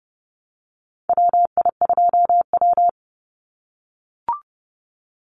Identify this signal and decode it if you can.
Morse code, heard as a keyed tone, about 30 wpm, 700 Hz. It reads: WI2W
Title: NA SOTA Chaser Callsigns 30wpm